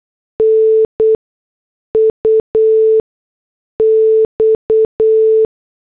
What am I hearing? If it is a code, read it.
Morse code, heard as a keyed tone, about 8 wpm, spaced as for 6 wpm, 440 Hz. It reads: NUX
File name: morse / headlines